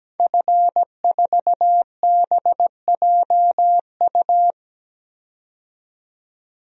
Morse code, heard as a keyed tone, about 17 wpm, 700 Hz. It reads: F4BJU
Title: DXCC-100 Call Signs - 5 Characters - 17wpm